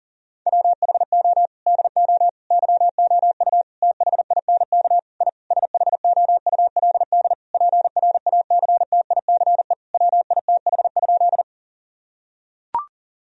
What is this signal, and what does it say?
Morse code, heard as a keyed tone, about 40 wpm, 700 Hz. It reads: WHO DO YOU THINK I SHOULD PRACTICE WITH?
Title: Sentences from Top 1000 Words - Review Part 4 - 40wpm